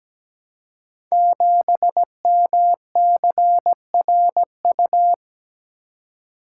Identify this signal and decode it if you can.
Morse code, heard as a keyed tone, about 17 wpm, 700 Hz. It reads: 7MCRU
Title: Code Groups - 5 Letter:Numbers - 17wpm